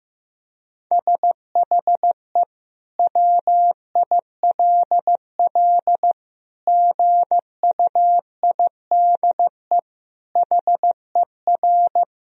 Morse code, heard as a keyed tone, about 15 wpm, 700 Hz. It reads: SHE WILL GUIDE HER
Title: Sentences from Top 1000 Words - Review Part 3 - 15wpm